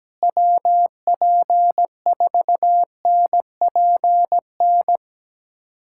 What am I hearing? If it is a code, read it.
Morse code, heard as a keyed tone, about 17 wpm, 700 Hz. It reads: WP4NPN
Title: USA Callsign DE Callsign 17wpm